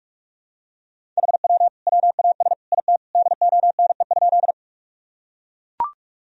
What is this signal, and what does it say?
Morse code, heard as a keyed tone, about 44 wpm, 700 Hz. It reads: HW WAS IT DONE?
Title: Sentences from Top 500 Words and Abbreviations 45wpm